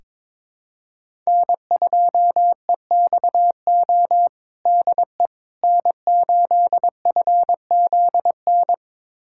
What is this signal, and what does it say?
Morse code, heard as a keyed tone, about 22 wpm, 700 Hz. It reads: N2EXO DE N8FZN
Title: USA Callsign DE Callsign - 5 Characters - 22wpm